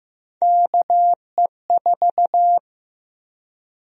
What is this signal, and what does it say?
Morse code, heard as a keyed tone, about 15 wpm, 700 Hz. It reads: KE4
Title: USA Call Sign Prefixes - 15wpm